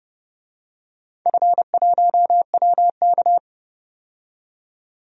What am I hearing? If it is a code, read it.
Morse code, heard as a keyed tone, about 30 wpm, 700 Hz. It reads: F1WK
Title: DXCC-100 Call Signs - 4 Characters - 30wpm